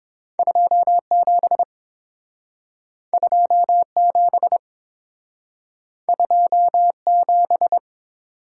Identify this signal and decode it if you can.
Morse code, heard as a keyed tone, about 30 wpm, 700 Hz. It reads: 27 27 27